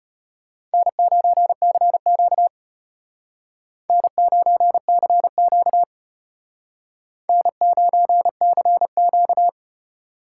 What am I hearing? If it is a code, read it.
Morse code, heard as a keyed tone, about 32 wpm, 700 Hz. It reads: N9CQ N9CQ N9CQ